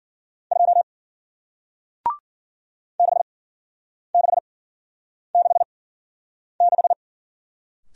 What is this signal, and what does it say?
Morse code, heard as a keyed tone, about 52 wpm, spaced as for 9 wpm, 700 Hz. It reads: 2 6666